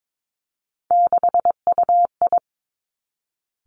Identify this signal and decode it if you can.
Morse code, heard as a keyed tone, about 22 wpm, 700 Hz. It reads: 6UI
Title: Three Challenging Letter-Number Combinations 22wpm